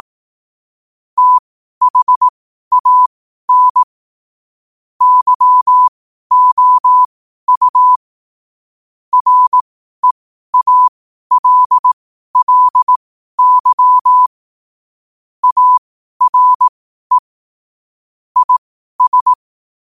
Morse code, heard as a keyed tone, about 18 wpm, spaced as for 12 wpm, 1000 Hz. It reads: THAN YOU REALLY ARE IS
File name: qotdfiles / 12wpm